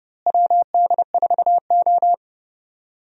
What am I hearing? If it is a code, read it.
Morse code, heard as a keyed tone, about 31 wpm, 700 Hz. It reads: WD4O
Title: USA Call Signs - 4 Characters - Repeated 3x - 30wpm